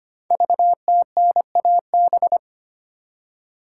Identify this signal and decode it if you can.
Morse code, heard as a keyed tone, about 25 wpm, 700 Hz. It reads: VTNAB
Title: Code Groups - 5 Letters - 25wpm